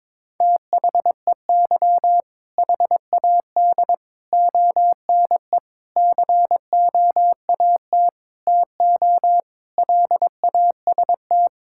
Morse code, heard as a keyed tone, about 22 wpm, 700 Hz. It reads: THEY HAD ONE COAT TO LAST
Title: Sentences from Top 1600 Words 22wpm